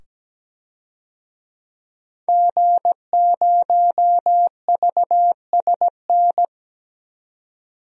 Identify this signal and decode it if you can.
Morse code, heard as a keyed tone, about 17 wpm, 700 Hz. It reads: G0VSN
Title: DXCC-100 Call Signs - 5 Characters - 17wpm